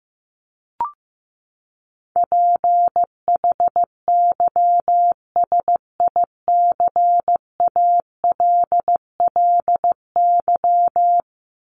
Morse code, heard as a keyed tone, about 15 wpm, 700 Hz. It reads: PHYSICALLY
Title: Top 1501-1750 Words 15wpm